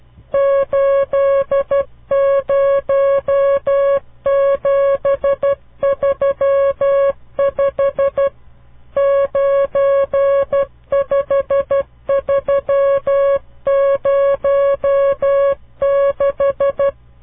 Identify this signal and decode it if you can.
Morse code, heard as a keyed tone, about 12 wpm, 550 Hz. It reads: 80735 95306
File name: CW Learning Programs / farnsworth / unit 5